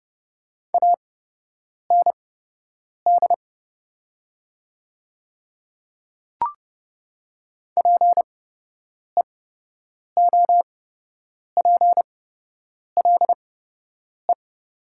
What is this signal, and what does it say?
Morse code, heard as a keyed tone, about 29 wpm, spaced as for 8 wpm, 700 Hz. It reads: AND PEOPLE
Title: 8x Character Spacing - Top 200 Words - 30wpm